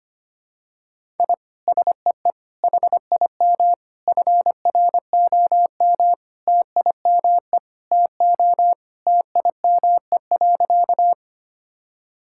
Morse code, heard as a keyed tone, about 25 wpm, 700 Hz. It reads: I SEE HIM FROM TIME TO TIME.
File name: practice / 25wpm / Sentences with 4-Word Phrases - 25wpm